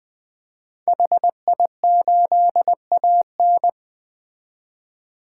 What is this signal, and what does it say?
Morse code, heard as a keyed tone, about 20 wpm, 700 Hz. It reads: HI8AN
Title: 2023 POTA Activator Call Signs - 20wpm